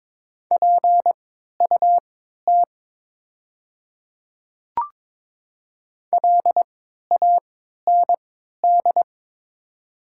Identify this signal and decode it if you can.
Morse code, heard as a keyed tone, about 22 wpm, spaced as for 12 wpm, 700 Hz. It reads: PUT LAND
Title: 3x Character Spacing - Top 300 Words - 22wpm